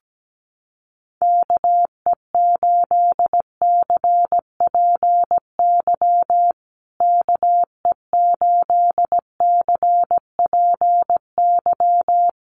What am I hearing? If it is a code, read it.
Morse code, heard as a keyed tone, about 17 wpm, 700 Hz. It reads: KE8CPY KE8CPY
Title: USA Call Signs Repeated 3x 17wpm